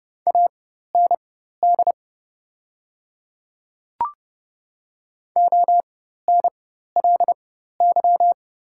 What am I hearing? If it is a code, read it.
Morse code, heard as a keyed tone, about 30 wpm, spaced as for 14 wpm, 700 Hz. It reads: AND ONLY
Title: 4x Character Spacing - Top 300 Words - 30wpm